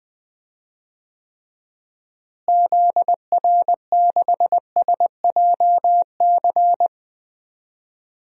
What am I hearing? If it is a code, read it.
Morse code, heard as a keyed tone, about 20 wpm, 700 Hz. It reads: ZR6SJC
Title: DXCC-100 Call Signs - 6 Characters - Repeated 3x - 20wpm